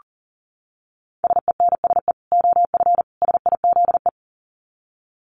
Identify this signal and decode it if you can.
Morse code, heard as a keyed tone, about 40 wpm, 700 Hz. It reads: SENSE OF SIZE
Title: Dit Challenge 40wpm